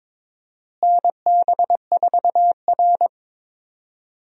Morse code, heard as a keyed tone, about 22 wpm, 700 Hz. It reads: NB4R
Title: USA Call Signs - 4 Characters - Repeated 2x - 22wpm